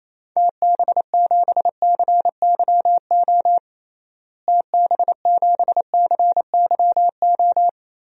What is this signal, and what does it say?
Morse code, heard as a keyed tone, about 28 wpm, 700 Hz. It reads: TB7CYO TB7CYO
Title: DXCC-100 Call Signs - 6 Characters - Repeated 3x - 28wpm